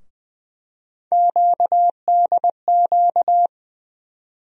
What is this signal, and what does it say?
Morse code, heard as a keyed tone, about 20 wpm, 700 Hz. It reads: QDQ
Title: USA Call Sign Postfixes - 3 Letters 20wpm